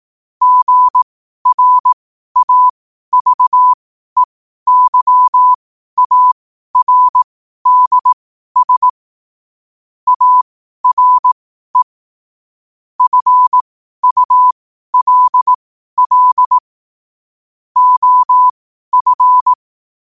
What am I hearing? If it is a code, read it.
Morse code, heard as a keyed tone, about 18 wpm, spaced as for 12 wpm, 1000 Hz. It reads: GRAVEYARDS ARE FULL OF
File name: qotdfiles / 12wpm